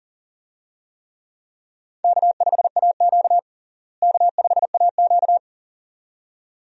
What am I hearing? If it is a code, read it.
Morse code, heard as a keyed tone, about 37 wpm, 700 Hz. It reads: K5AQ K5AQ
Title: USA Call Signs - 4 Characters - Repeated 2x - 40wpm